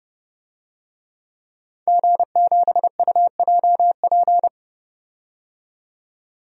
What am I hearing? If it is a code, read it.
Morse code, heard as a keyed tone, about 30 wpm, 700 Hz. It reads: G7UJP
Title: DXCC-100 Call Signs - 5 Characters - 30wpm